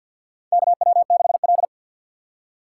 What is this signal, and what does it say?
Morse code, heard as a keyed tone, about 53 wpm, 700 Hz. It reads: KW6L
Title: USA Call Signs - 4 Characters - 6x Character Spacing - 50wpm